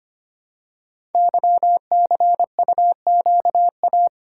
Morse code, heard as a keyed tone, about 25 wpm, 700 Hz. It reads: YCUQA
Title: Code Groups - 5 Letters - 25wpm